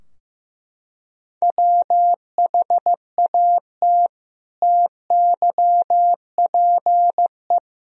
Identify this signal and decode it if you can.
Morse code, heard as a keyed tone, about 15 wpm, 700 Hz. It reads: WHAT TYPE
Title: Sentences from Top 1000 Words - Review Part 3 - 15wpm